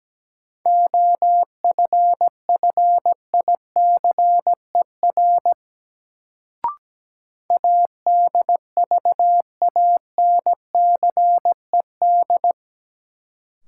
Morse code, heard as a keyed tone, about 17 wpm, 700 Hz. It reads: OFFICER ADVANCED
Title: Sentences from Top 1200 Words - New Word Drill Alt - 17wpm